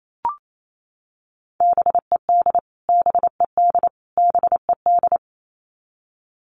Morse code, heard as a keyed tone, about 28 wpm, 700 Hz. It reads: BED BED BED
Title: Top 500 Words Repeated 28wpm